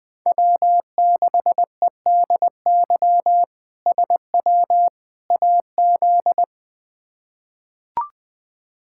Morse code, heard as a keyed tone, about 21 wpm, 700 Hz. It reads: W6EDY SW AZ
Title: K1USN SST QSO Practice - 20wpm